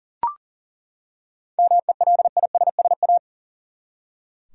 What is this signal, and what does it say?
Morse code, heard as a keyed tone, about 40 wpm, 700 Hz. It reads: MELISSA
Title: USA Common Names - 40wpm